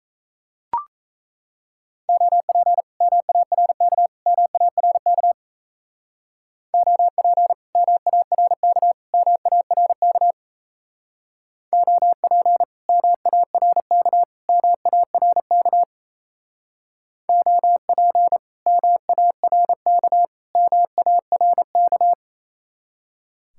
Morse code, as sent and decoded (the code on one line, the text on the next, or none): --- .--. / -- .- .-. -.- / -- .- .-. -.- / --- .--. / -- .- .-. -.- / -- .- .-. -.- / --- .--. / -- .- .-. -.- / -- .- .-. -.- / --- .--. / -- .- .-. -.- / -- .- .-. -.-
OP MARK MARK OP MARK MARK OP MARK MARK OP MARK MARK